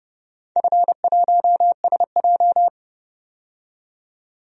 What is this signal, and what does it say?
Morse code, heard as a keyed tone, about 30 wpm, 700 Hz. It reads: F1SJ
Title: DXCC-100 Call Signs - 4 Characters - 30wpm